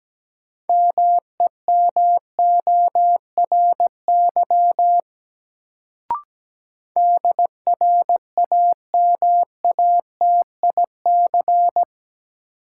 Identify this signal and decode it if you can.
Morse code, heard as a keyed tone, about 17 wpm, 700 Hz. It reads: MEMORY DRAMATIC
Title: Sentences from Top 1300 Words - New Word Drill Alt - 17wpm